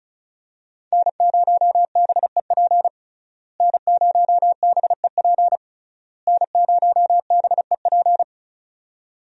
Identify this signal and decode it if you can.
Morse code, heard as a keyed tone, about 36 wpm, 700 Hz. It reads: N0BEP N0BEP N0BEP
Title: USA Call Signs - 5 Characters - Repeated 3x - 35wpm